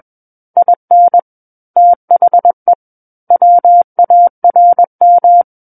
Morse code, heard as a keyed tone, about 21 wpm, 700 Hz. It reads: IN THE WARM
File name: Book One / 12-Chapter 12 - 21 WPM